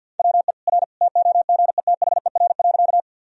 Morse code, heard as a keyed tone, about 50 wpm, 700 Hz. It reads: WE R TOGETHER.
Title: Sentences from Top 300 Words and Abbreviations 50wpm